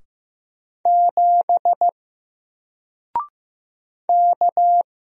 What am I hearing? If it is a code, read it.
Morse code, heard as a keyed tone, about 15 wpm, 700 Hz. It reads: 7 K
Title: Single Character (Letter:Number:Punctuation:Prosign) Alt 15wpm